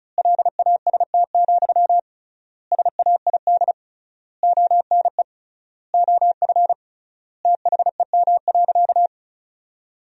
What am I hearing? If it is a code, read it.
Morse code, heard as a keyed tone, about 36 wpm, 700 Hz. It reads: LAST, SAID ONE OF THEM.
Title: 3x Word-Spacing - Sentences from Top 300 Words - 35wpm